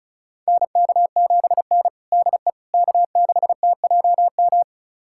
Morse code, heard as a keyed tone, about 36 wpm, 700 Hz. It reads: NK7N DE K6TJM
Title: USA Callsign DE Callsign 35wpm